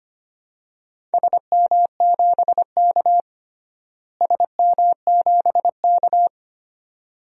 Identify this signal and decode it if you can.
Morse code, heard as a keyed tone, about 25 wpm, 700 Hz. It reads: SM7K SM7K
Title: DXCC-100 Call Signs - 4 Characters - Repeated 2x - 25wpm